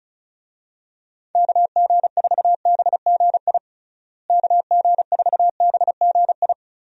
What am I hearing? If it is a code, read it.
Morse code, heard as a keyed tone, about 34 wpm, 700 Hz. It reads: KG4BGI KG4BGI
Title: USA Call Signs - 6 Characters - Repeated 2x - 35wpm